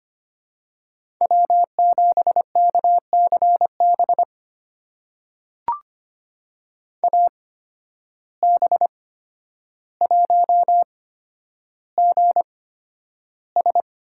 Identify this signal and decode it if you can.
Morse code, heard as a keyed tone, about 25 wpm, spaced as for 11 wpm, 700 Hz. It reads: W7KCB AB1GS
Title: USA Call Signs - 8x Character Spacing - 25wpm